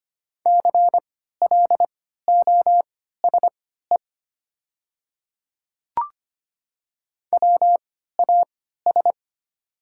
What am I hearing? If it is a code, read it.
Morse code, heard as a keyed tone, about 25 wpm, spaced as for 14 wpm, 700 Hz. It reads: CLOSE WAS
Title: 3x Character Spacing - Top 300 Words - 25wpm